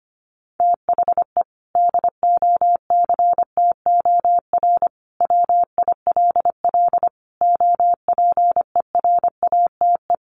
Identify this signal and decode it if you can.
Morse code, heard as a keyed tone, about 25 wpm, 700 Hz. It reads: THE DOCTOR WILL OPERATE
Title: Sentences from Top 1500 Words 25wpm